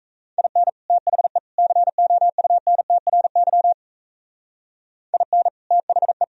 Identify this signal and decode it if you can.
Morse code, heard as a keyed tone, about 34 wpm, 700 Hz. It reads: IN THE COUNTRY IN THE
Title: Speed-Racing - Sets of 3 Words 28wpm